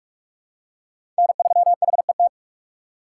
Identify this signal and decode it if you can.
Morse code, heard as a keyed tone, about 41 wpm, 700 Hz. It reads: N3HET
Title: USA Call Signs - 5 Characters - 3x Character Spacing - 45wpm